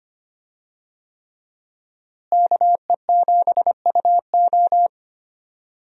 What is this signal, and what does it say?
Morse code, heard as a keyed tone, about 25 wpm, 700 Hz. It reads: KE7UO
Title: USA Call Signs - 5 Characters - Repeated 3x - 25wpm